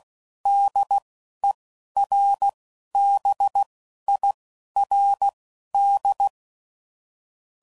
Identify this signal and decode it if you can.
Morse code, heard as a keyed tone, about 16 wpm, spaced as for 11 wpm, 800 Hz. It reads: DERBIRD
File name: CW Code Groups @15WPM / QSOs / WD0BGP Spaced